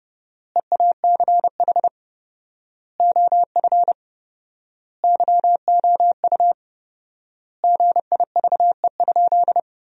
Morse code, heard as a keyed tone, about 30 wpm, 700 Hz. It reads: EACH OF YOU GIVE?
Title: 4x Word-Spacing - Sentences from Top 200 Words - 30wpm